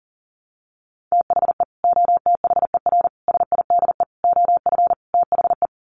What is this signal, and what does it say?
Morse code, heard as a keyed tone, about 40 wpm, 700 Hz. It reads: THE OTHER SIDE OF THE
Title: Sets of 5 Words 40wpm